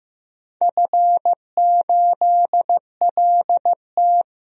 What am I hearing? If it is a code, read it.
Morse code, heard as a keyed tone, about 15 wpm, 700 Hz. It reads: F8LT
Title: DXCC-100 Call Signs - 4 Characters - Repeated 2x - 15wpm